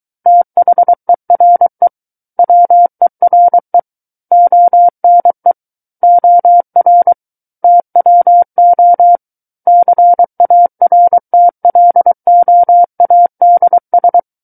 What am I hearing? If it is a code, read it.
Morse code, heard as a keyed tone, about 23 wpm, 700 Hz. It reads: THERE WERE ONE OR TWO CARTLOADS
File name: Book One / 14-Chapter 14 - 23 WPM